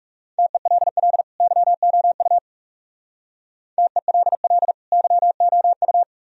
Morse code, heard as a keyed tone, about 38 wpm, 700 Hz. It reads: TELL YOU TELL YOU